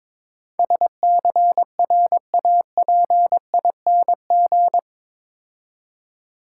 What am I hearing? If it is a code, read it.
Morse code, heard as a keyed tone, about 22 wpm, 690 Hz. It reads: SCRAPING
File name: Easy Words - Speed-Racing with Hints - 22wpm